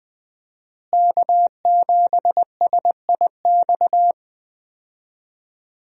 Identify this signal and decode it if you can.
Morse code, heard as a keyed tone, about 20 wpm, 700 Hz. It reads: K7SIX
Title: USA Call Signs - 5 Characters - Repeated 2x - 20wpm